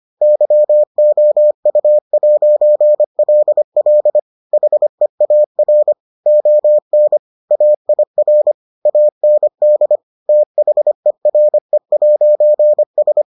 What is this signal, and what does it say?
Morse code, heard as a keyed tone, about 25 wpm, 600 Hz. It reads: YOU'LL HEAR ON AIR AND THERE'S